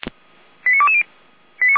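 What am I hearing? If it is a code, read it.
Five-tone selective call (ZVEI), heard as quick runs of five tones, tones of about 70 ms. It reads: 89220
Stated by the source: ZVEI five-tone selective call received with an SDR, on a Feuerwehr Orlamünde server.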